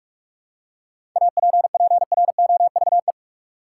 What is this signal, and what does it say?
Morse code, heard as a keyed tone, about 44 wpm, 700 Hz. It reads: APPROVE
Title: Seven Letter Words - 45wpm